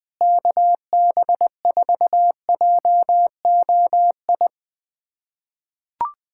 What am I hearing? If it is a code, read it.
Morse code, heard as a keyed tone, about 20 wpm, 700 Hz. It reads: KB4JOI
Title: USA Callsign DE Callsign - 6 Characters - 20wpm